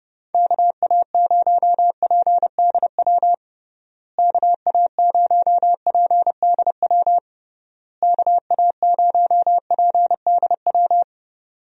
Morse code, heard as a keyed tone, about 30 wpm, 700 Hz. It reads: KA0PDW KA0PDW KA0PDW
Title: USA Call Signs - 6 Characters - Repeated 3x - 30wpm